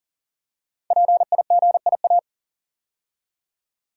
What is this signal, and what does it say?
Morse code, heard as a keyed tone, about 43 wpm, 700 Hz. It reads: PIGIA
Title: Code Groups - 5 Letters - 40wpm